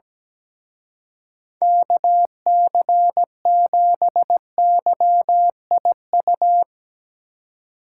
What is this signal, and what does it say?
Morse code, heard as a keyed tone, about 18 wpm, 700 Hz. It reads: KC7YIU
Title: Speed Racing - USA 2x3 Call Signs 17wpm